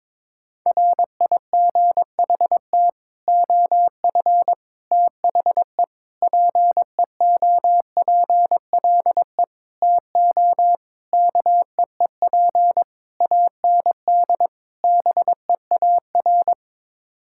Morse code, heard as a keyed tone, about 22 wpm, 700 Hz. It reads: RIGHT OF THE PEOPLE TO KEEP AND BEAR
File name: practice / 22wpm / Sets of 8 Words - 22wpm